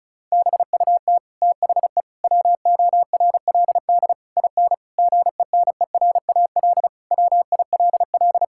Morse code, heard as a keyed tone, about 36 wpm, 700 Hz. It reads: BUT THE WORLD IN GENERAL WILL
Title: Sentences from Top 1000 Words - Review Part 6 - 35wpm